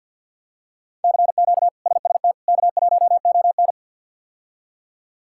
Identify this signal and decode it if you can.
Morse code, heard as a keyed tone, about 49 wpm, 700 Hz. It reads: CQ SST K1ON
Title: K1USN SST QSO Practice - 50wpm